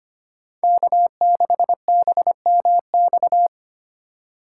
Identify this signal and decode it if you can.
Morse code, heard as a keyed tone, about 25 wpm, 700 Hz. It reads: K6BMX
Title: USA Call Signs - 5 Characters - 3x Character Spacing - 25wpm